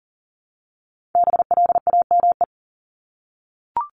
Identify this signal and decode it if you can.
Morse code, heard as a keyed tone, about 40 wpm, 700 Hz. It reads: BLAME
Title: Top 1751-2000 Words 40wpm